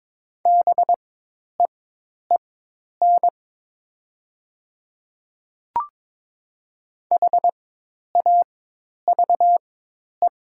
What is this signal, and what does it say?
Morse code, heard as a keyed tone, about 22 wpm, spaced as for 10 wpm, 700 Hz. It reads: BEEN HAVE
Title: 4x Character Spacing - Top 100 Words - 22wpm